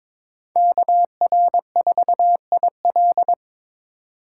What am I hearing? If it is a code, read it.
Morse code, heard as a keyed tone, about 22 wpm, 700 Hz. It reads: KR4IL
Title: USA Call Signs - 5 Characters - 2x Character Spacing - 22wpm